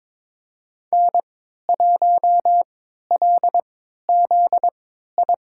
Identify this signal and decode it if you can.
Morse code, heard as a keyed tone, about 21 wpm, spaced as for 12 wpm, 700 Hz. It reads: N1LZI
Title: USA Call Signs - 5 Characters - 3x Character Spacing - 22wpm